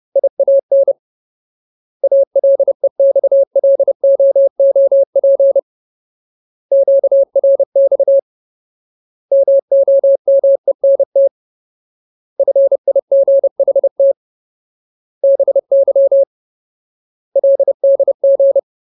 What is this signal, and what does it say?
Morse code, heard as a keyed tone, about 30 wpm, 550 Hz. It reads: IAN ALEXLOOP QRX MOMENT FIGHT BY LDG
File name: Ultimate Word List / 30WPM Ultimate